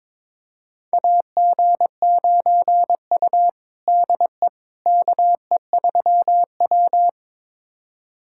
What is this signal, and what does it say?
Morse code, heard as a keyed tone, about 22 wpm, 700 Hz. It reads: AG9U DE KE3W
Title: USA Callsign DE Callsign - 4 Characters - 22wpm